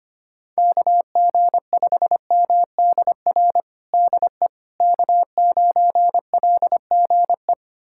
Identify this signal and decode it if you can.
Morse code, heard as a keyed tone, about 25 wpm, 700 Hz. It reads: KG5MDR DE K9LGE
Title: USA Callsign DE Callsign 25wpm